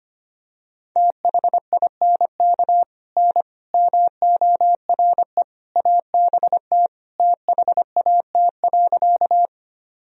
Morse code, heard as a keyed tone, about 25 wpm, 700 Hz. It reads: THINK N MORE ABT THAT.
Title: Sentences from Top 200 Words and Abbreviations 25wpm